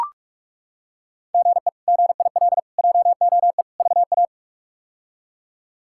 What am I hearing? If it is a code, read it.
Morse code, heard as a keyed tone, about 42 wpm, 700 Hz. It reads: GE GIL JOE VA
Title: K1USN SST QSO Practice - 45wpm